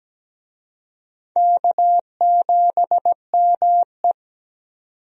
Morse code, heard as a keyed tone, about 17 wpm, 700 Hz. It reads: K7ME
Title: USA Call Signs - 4 Characters - 17wpm